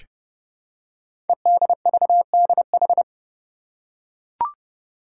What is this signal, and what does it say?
Morse code, heard as a keyed tone, about 28 wpm, 700 Hz. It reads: EDVDH
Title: Code Groups - 5 Letters - 30wpm